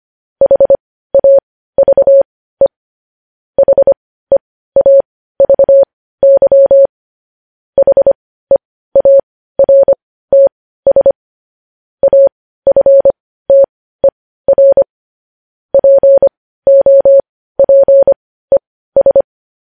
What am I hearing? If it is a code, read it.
Morse code, heard as a keyed tone, about 25 wpm, spaced as for 15 wpm, 550 Hz. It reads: HAVE HEAVY HEARTS AFTER POPES
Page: News Headlines in Morse Code at 15 WPM